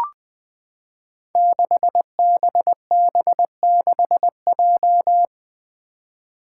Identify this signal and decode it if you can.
Morse code, heard as a keyed tone, about 20 wpm, 700 Hz. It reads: 6BB6J
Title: Five Challenging Letter-Number Combinations 20wpm